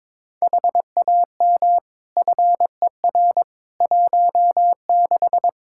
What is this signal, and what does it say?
Morse code, heard as a keyed tone, about 21 wpm, 700 Hz. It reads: HAM FER 16
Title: Practice QSO - Exchange 3 - More Info - 22wpm